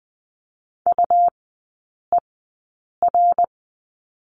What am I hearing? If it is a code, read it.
Morse code, heard as a keyed tone, about 20 wpm, spaced as for 8 wpm, 700 Hz. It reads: UER
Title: 138 Lesson 23e - Letter f - Three Characters 20wpm